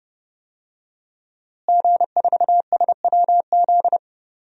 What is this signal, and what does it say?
Morse code, heard as a keyed tone, about 30 wpm, 700 Hz. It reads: G4SWZ
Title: DXCC-100 Call Signs - 5 Characters - Repeated 3x - 30wpm